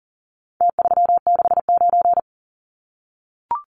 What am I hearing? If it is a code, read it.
Morse code, heard as a keyed tone, about 40 wpm, 700 Hz. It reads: T369
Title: Numbers - Contest Style - 4-Digit 40wpm